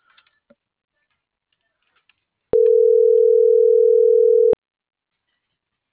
Ringback tone (North America), 440 + 480 Hz.
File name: CENTRAL MULTIMODAL / out josimar.cruz - .59803